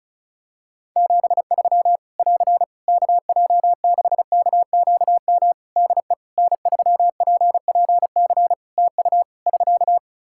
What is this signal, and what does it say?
Morse code, heard as a keyed tone, about 36 wpm, 700 Hz. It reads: 73 <AR> KJ6KQM DE N3PPC TU <SK>
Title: K1USN SST QSO Practice 35wpm